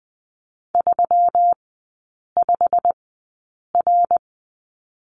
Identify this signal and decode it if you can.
Morse code, heard as a keyed tone, about 20 wpm, spaced as for 8 wpm, 700 Hz. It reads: 35R
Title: 138 Lesson 23e - Letter f - Three Characters 20wpm